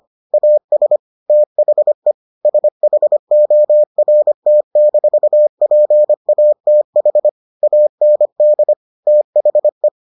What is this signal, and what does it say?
Morse code, heard as a keyed tone, about 25 wpm, 600 Hz. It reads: AS THE SHORT-PATH AND THE